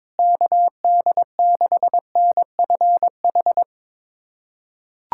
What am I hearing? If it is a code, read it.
Morse code, heard as a keyed tone, about 22 wpm, 700 Hz. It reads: KD6NFH
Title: USA Callsign DE Callsign - 6 Characters - 22wpm